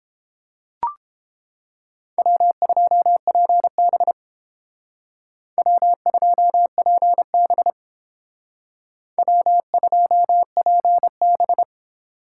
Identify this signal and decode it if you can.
Morse code, heard as a keyed tone, about 31 wpm, 700 Hz. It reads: W2PB W2PB W2PB